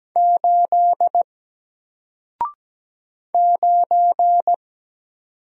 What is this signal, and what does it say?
Morse code, heard as a keyed tone, about 17 wpm, 700 Hz. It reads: 8 9
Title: Single Letter-Number 17wpm